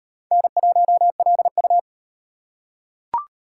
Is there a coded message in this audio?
Yes: Morse code from 0.3 to 1.8 s, heard as a keyed tone, about 33 wpm, 700 Hz. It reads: N1LU